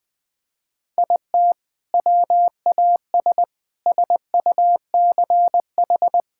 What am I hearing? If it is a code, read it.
Morse code, heard as a keyed tone, about 20 wpm, 700 Hz. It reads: IT WAS SUCH
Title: Sentences with 2-Word Phrases - 20wpm